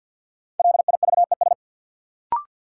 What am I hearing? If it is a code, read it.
Morse code, heard as a keyed tone, about 53 wpm, 700 Hz. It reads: LIVES
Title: Sets of 5 Words Encore - New Word Drill - 50wpm